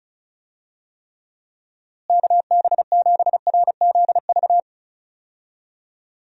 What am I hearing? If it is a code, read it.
Morse code, heard as a keyed tone, about 36 wpm, 700 Hz. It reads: KB7RZV
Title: USA Call Signs - 6 Characters - Repeated 2x - 35wpm